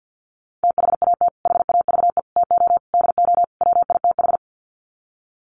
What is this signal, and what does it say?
Morse code, heard as a keyed tone, about 49 wpm, 700 Hz. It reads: THAT HAVE TO DO WITH
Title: Sets of 5 Words Encore2 50wpm